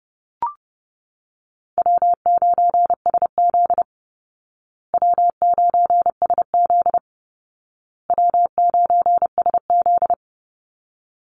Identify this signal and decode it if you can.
Morse code, heard as a keyed tone, about 30 wpm, 700 Hz. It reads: W9SZ W9SZ W9SZ
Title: Easy Call Signs Repeated 3x - 30wpm